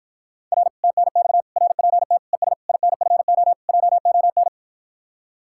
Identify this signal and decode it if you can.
Morse code, heard as a keyed tone, about 52 wpm, 700 Hz. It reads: R TNX RPT ES INFO JON